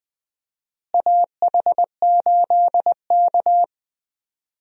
Morse code, heard as a keyed tone, about 20 wpm, 700 Hz. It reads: AH8K
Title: USA Call Signs - 4 Characters - Repeated 3x - 20wpm